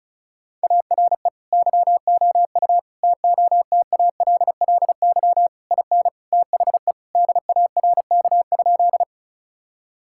Morse code, heard as a keyed tone, about 34 wpm, 700 Hz. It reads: ARE YOU TOTALLY IN THE DARK?
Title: Sentences from Top 1600 Words 35wpm